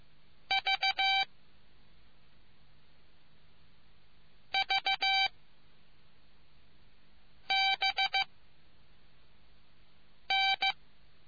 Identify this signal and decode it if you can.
Morse code, heard as a keyed tone, about 15 wpm, 780 Hz. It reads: V V B N